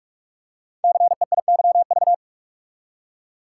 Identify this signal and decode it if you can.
Morse code, heard as a keyed tone, about 44 wpm, 700 Hz. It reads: CEIYV
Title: Code Groups - 5 Letter:Numbers - 45wpm